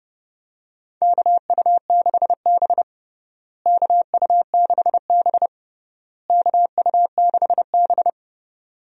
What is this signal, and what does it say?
Morse code, heard as a keyed tone, about 29 wpm, 700 Hz. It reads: KU6B KU6B KU6B
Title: USA Call Signs - 4 Characters - Repeated 3x - 30wpm